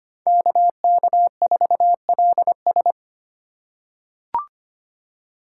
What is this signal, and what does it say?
Morse code, heard as a keyed tone, about 25 wpm, 700 Hz. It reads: KK4LS
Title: USA Call Signs - 5 Characters - 2x Character Spacing - 25wpm